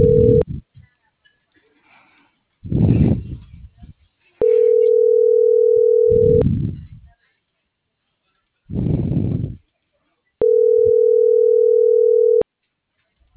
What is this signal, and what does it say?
Ringback tone (North America), 440 + 480 Hz.